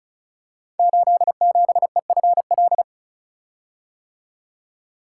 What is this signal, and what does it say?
Morse code, heard as a keyed tone, about 36 wpm, 700 Hz. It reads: 87EFL
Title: Code Groups - 5 Letter:Numbers - 35wpm